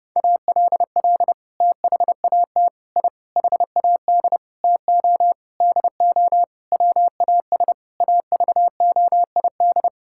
Morse code, heard as a keyed tone, about 32 wpm, 700 Hz. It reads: ALL THAT I HAD TO DO WAS AVOID
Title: Sentences from Top 1000 Words - Review Part 6 - 30wpm